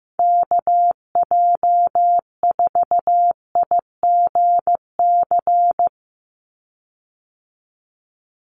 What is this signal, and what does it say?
Morse code, heard as a keyed tone, about 15 wpm, 700 Hz. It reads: KJ4IGC
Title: USA Call Signs 15wpm